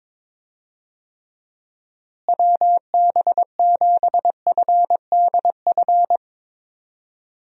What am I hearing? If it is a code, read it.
Morse code, heard as a keyed tone, about 22 wpm, 700 Hz. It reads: WB7FDF
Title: USA Call Signs - 6 Characters - Repeated 2x - 22wpm